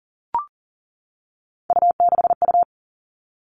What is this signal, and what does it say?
Morse code, heard as a keyed tone, about 40 wpm, 700 Hz. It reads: U6U
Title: All Cut Numbers - 3-Digit 40wpm